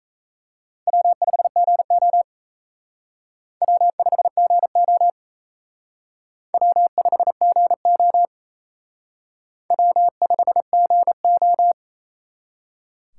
Morse code, heard as a keyed tone, about 34 wpm, 700 Hz. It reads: W5GO W5GO W5GO W5GO
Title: Speed Racing - USA Call Signs 28wpm